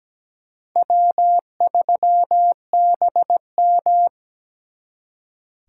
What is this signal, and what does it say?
Morse code, heard as a keyed tone, about 20 wpm, 700 Hz. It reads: W3BM
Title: Speed Racing - USA 1x2 Call Signs 17wpm